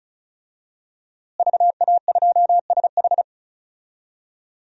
Morse code, heard as a keyed tone, about 36 wpm, 700 Hz. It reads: VA2SH
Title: DXCC-100 Call Signs - 5 Characters - Repeated 3x - 35wpm